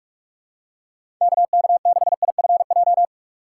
Morse code, heard as a keyed tone, about 44 wpm, 700 Hz. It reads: KK6IFJ
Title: USA Call Signs - 6 Characters - 6x Character Spacing - 45wpm